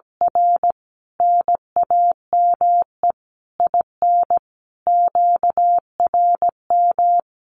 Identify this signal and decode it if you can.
Morse code, heard as a keyed tone, about 17 wpm, 700 Hz. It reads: R NAME IN QRM
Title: Common QSO Phrases 17wpm